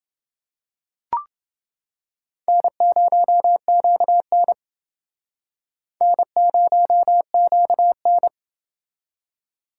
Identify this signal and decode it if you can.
Morse code, heard as a keyed tone, about 29 wpm, 700 Hz. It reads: N0QN N0QN